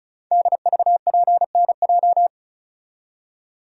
Morse code, heard as a keyed tone, about 36 wpm, 700 Hz. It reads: DVPNJ
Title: Code Groups - 5 Letters - 35wpm